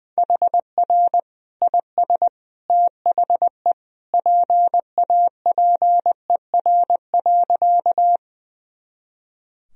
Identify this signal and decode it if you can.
Morse code, heard as a keyed tone, about 20 wpm, 700 Hz. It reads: HR IS THE PAPER.
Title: Sentences from Top 300 Words and Abbreviations 20wpm